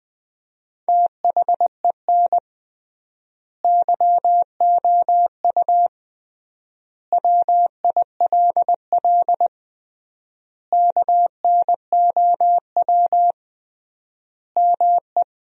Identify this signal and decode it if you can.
Morse code, heard as a keyed tone, about 21 wpm, 700 Hz. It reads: THEN YOU WILL KNOW ME
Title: 3x Word-Spacing - Sentences from Top 400 Words - 20wpm